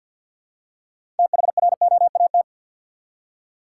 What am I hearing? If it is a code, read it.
Morse code, heard as a keyed tone, about 49 wpm, 700 Hz. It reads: THROAT
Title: Six Letter Words - 50wpm